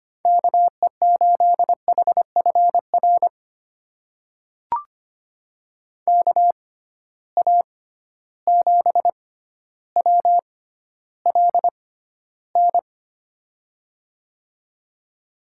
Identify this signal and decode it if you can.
Morse code, heard as a keyed tone, about 25 wpm, spaced as for 13 wpm, 700 Hz. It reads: KE8HFR KA7WLN
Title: USA Call Signs - 6 Characters - 6x Character Spacing - 25wpm